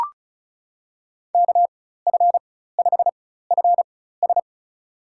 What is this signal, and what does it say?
Morse code, heard as a keyed tone, about 34 wpm, spaced as for 16 wpm, 700 Hz. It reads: KF5FS
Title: USA Call Signs - 5 Characters - 4x Character Spacing - 35wpm